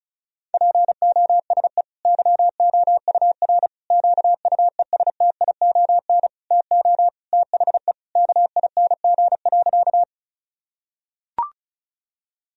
Morse code, heard as a keyed tone, about 36 wpm, 700 Hz. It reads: POSE YOUR QUESTION TO THE KING.
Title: Sentences from Top 1000 Words - Review Part 2 - 35wpm